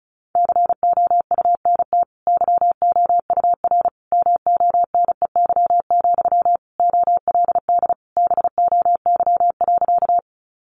Morse code, heard as a keyed tone, about 34 wpm, 700 Hz. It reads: COUNT YOUR MONEY, OLD BOY.
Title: Sentences from Top 1500 Words 35wpm